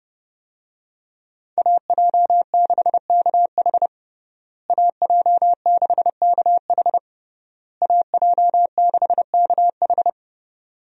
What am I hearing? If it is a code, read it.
Morse code, heard as a keyed tone, about 30 wpm, 700 Hz. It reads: AJ6KH AJ6KH AJ6KH
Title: USA Call Signs - 5 Characters - Repeated 3x - 30wpm